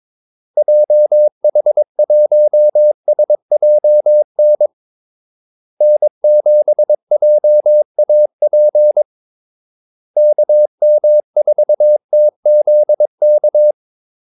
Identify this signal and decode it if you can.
Morse code, heard as a keyed tone, about 22 wpm, 600 Hz. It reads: JH1SJN N7JAP KM4TZK
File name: Callsigns-22wpm